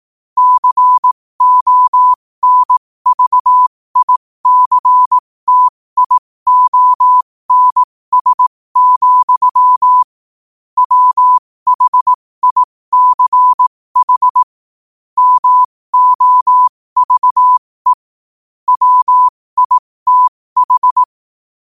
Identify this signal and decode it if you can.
Morse code, heard as a keyed tone, about 18 wpm, spaced as for 15 wpm, 1000 Hz. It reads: CONVICTIONS, WHICH MOVE WITH
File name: qotdfiles / 15wpm